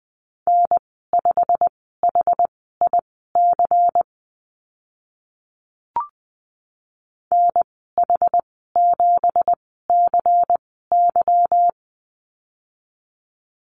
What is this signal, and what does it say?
Morse code, heard as a keyed tone, about 20 wpm, spaced as for 14 wpm, 700 Hz. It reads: N5HIC NH7CY
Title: 167 Lesson 27g - Number 7 - Call Signs 20wpm